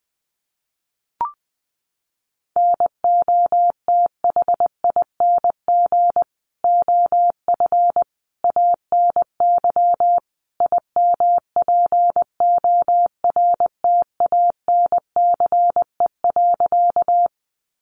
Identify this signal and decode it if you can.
Morse code, heard as a keyed tone, about 20 wpm, 700 Hz. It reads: NOTHING OF ANY IMPORTANCE.
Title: Sentences from Top 1200 Words 20wpm